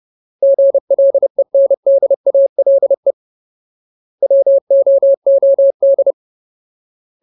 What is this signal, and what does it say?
Morse code, heard as a keyed tone, about 30 wpm, 550 Hz. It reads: GLENDALE WOOD
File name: Ultimate Word List / 30WPM Ultimate